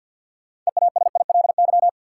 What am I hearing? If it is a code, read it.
Morse code, heard as a keyed tone, about 50 wpm, 700 Hz. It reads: EASILY
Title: Sentences from Top 900 Words - New Word Drill - 50wpm